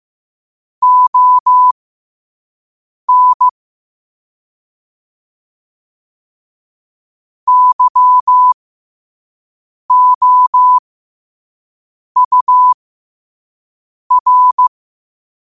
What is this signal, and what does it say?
Morse code, heard as a keyed tone, about 15 wpm, spaced as for 5 wpm, 1000 Hz. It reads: ON YOUR
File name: qotdfiles / 5wpm